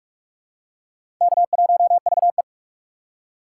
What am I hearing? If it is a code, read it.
Morse code, heard as a keyed tone, about 46 wpm, 700 Hz. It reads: K1VE
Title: USA Call Signs - 4 Characters - 4x Character Spacing - 45wpm